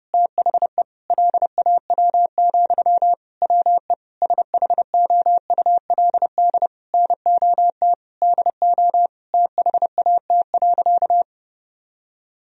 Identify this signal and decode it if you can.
Morse code, heard as a keyed tone, about 30 wpm, 700 Hz. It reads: THE LAW, WE SHOULD NOT DO THAT.
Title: Sentences from Top 1000 Words - Review Part 4 - 30wpm